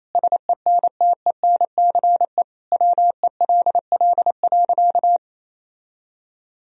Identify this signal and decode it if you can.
Morse code, heard as a keyed tone, about 28 wpm, 700 Hz. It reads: SENTENCE WELL.
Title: Sentences from Top 1000 Words - Review Part 1 - 28wpm